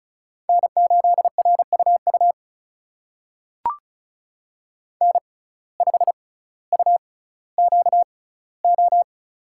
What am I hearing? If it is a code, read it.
Morse code, heard as a keyed tone, about 36 wpm, spaced as for 18 wpm, 700 Hz. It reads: N8RUU N5UQO
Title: USA Call Signs - 5 Characters - 6x Character Spacing - 35wpm